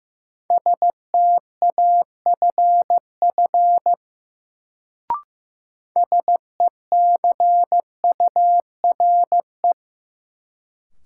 Morse code, heard as a keyed tone, about 15 wpm, 700 Hz. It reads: STAFF SECURE
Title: Sentences from Top 1200 Words - New Word Drill - 15wpm